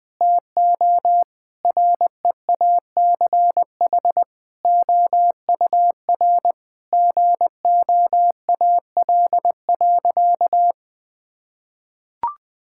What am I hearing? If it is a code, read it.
Morse code, heard as a keyed tone, about 20 wpm, 700 Hz. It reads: TO REACH OUR GOAL.
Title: Sentences with 7-Word Phrases - 20wpm